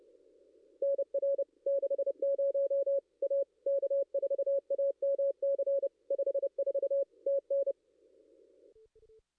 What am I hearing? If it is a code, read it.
Morse code, heard as a keyed tone, about 30 wpm, 550 Hz. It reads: NR 60 A K4AMC 54 TN
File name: contests / ss-cw-k5tr-op-n5rz -2200